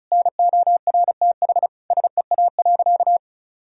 Morse code, heard as a keyed tone, about 34 wpm, 700 Hz. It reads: NORTH SEA.
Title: Sentences from Top 1000 Words - Review Part 1 - 35wpm